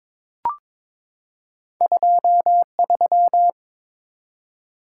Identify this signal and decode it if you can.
Morse code, heard as a keyed tone, about 22 wpm, 700 Hz. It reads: 23
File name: Two Letter-Number Combinations - 22wpm